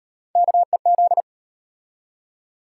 Morse code, heard as a keyed tone, about 34 wpm, 700 Hz. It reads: KEZ